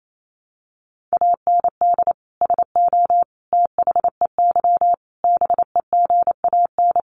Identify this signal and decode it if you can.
Morse code, heard as a keyed tone, about 28 wpm, 700 Hz. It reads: AND SO THEY BEGAN
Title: Sentences from Top 400 Words 28wpm